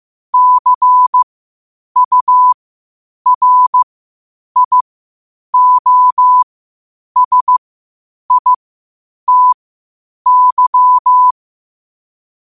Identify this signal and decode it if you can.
Morse code, heard as a keyed tone, about 15 wpm, spaced as for 8 wpm, 1000 Hz. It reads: CURIOSITY
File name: qotdfiles / 8wpm